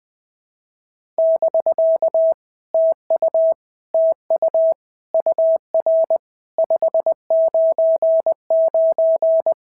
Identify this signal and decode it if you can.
Morse code, heard as a keyed tone, about 20 wpm, 650 Hz. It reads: <BK> TU TU UR 599
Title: POTA QSO Practice - 20wpm